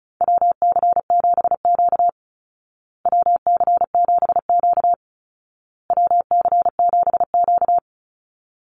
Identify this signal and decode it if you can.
Morse code, heard as a keyed tone, about 34 wpm, 700 Hz. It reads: WC7Q WC7Q WC7Q
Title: Call Signs Repeated 2x - 35wpm